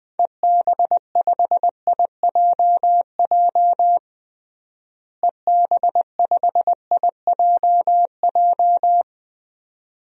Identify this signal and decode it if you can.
Morse code, heard as a keyed tone, about 21 wpm, 700 Hz. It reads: EB5IJJ EB5IJJ
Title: DXCC-100 Call Signs - 6 Characters - Repeated 2x - 20wpm